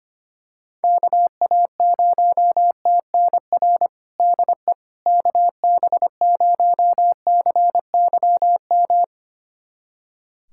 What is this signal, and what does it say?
Morse code, heard as a keyed tone, about 25 wpm, 700 Hz. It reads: KA0TNR DE KB0CYM
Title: USA Callsign DE Callsign 25wpm